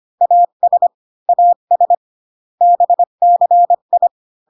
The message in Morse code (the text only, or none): AS AS BCI